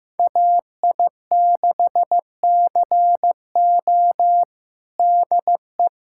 Morse code, heard as a keyed tone, about 15 wpm, 700 Hz. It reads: AI6CO DE
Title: USA Callsign DE Callsign 15wpm